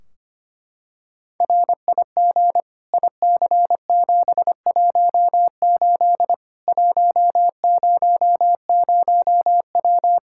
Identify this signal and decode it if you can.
Morse code, heard as a keyed tone, about 25 wpm, 700 Hz. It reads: RIG IC718 100W
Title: Practice QSO - Exchange 2 - Info - 25wpm